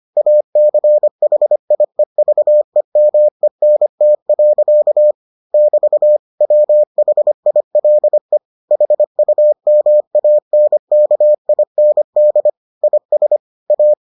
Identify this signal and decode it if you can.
Morse code, heard as a keyed tone, about 25 wpm, 600 Hz. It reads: ACHIEVEMENT. <BT> WHILE HUMANKIND IS A